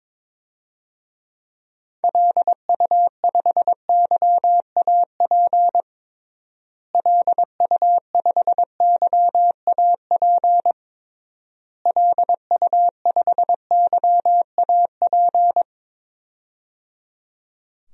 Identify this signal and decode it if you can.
Morse code, heard as a keyed tone, about 21 wpm, 700 Hz. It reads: LU5YAP LU5YAP LU5YAP
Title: DXCC-100 Call Signs - 6 Characters - Repeated 3x - 22wpm